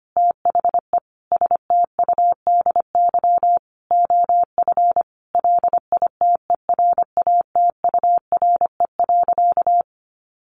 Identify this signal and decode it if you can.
Morse code, heard as a keyed tone, about 25 wpm, 700 Hz. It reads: THE STUDY OF LITERATURE.
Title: Sentences from Top 900 Words 25wpm